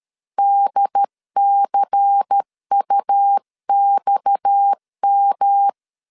Morse code, heard as a keyed tone, about 13 wpm, 800 Hz. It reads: DCUXM
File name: LEZIONI / 7 GF9Z / 7 G 12wpm 2055